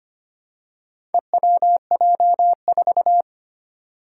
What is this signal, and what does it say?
Morse code, heard as a keyed tone, about 25 wpm, 700 Hz. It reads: EWJ4
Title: Four Challenging Letter-Number Combinations 25wpm